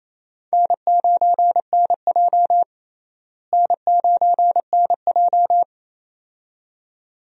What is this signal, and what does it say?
Morse code, heard as a keyed tone, about 27 wpm, 700 Hz. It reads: N9NJ N9NJ
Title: USA Call Signs - 4 Characters - Repeated 2x - 28wpm